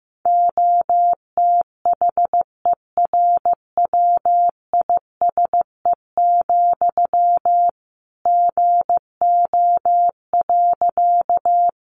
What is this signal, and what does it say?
Morse code, heard as a keyed tone, about 15 wpm, 700 Hz. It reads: OTHERWISE, GO.
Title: Sentences from Top 1200 Words 15wpm